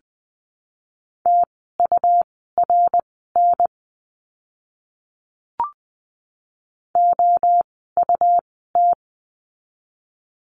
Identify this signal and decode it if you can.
Morse code, heard as a keyed tone, about 20 wpm, spaced as for 15 wpm, 700 Hz. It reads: TURN OUT
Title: 244 Lesson 39f - Letter z - Words 20wpm